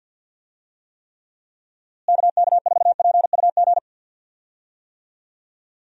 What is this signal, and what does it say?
Morse code, heard as a keyed tone, about 50 wpm, 700 Hz. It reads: KK4PUG
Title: USA Call Signs - 6 Characters - Repeated 2x - 50wpm